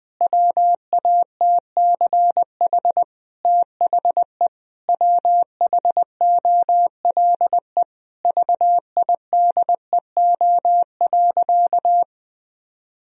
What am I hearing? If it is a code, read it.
Morse code, heard as a keyed tone, about 19 wpm, 700 Hz. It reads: WATCH THE WHOLE VIDEO.
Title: Sentences from Top 1000 Words - Review Part 4 - 20wpm